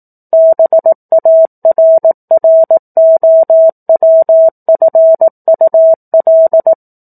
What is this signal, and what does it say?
Morse code, heard as a keyed tone, about 18 wpm, 650 Hz. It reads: BARROWFUL
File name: 04-Chapter-04 18 WPM